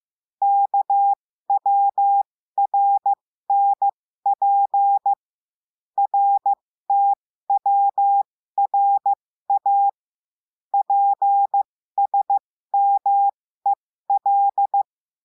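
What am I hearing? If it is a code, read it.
Morse code, heard as a keyed tone, about 15 wpm, spaced as for 13 wpm, 800 Hz. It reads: KWRNP RTWRA PSMEL
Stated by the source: Morse code lesson adding the letter W, with a five-letter-group copy drill.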